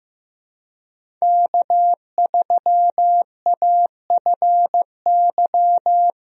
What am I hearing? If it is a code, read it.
Morse code, heard as a keyed tone, about 15 wpm, 700 Hz. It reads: K3AFY
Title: USA Call Signs - 5 Characters - Repeated 2x - 15wpm